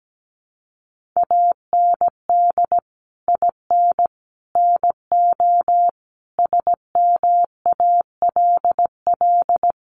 Morse code, heard as a keyed tone, about 17 wpm, 700 Hz. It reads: AND IN NO SMALL
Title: Sentences from Top 1500 Words 17wpm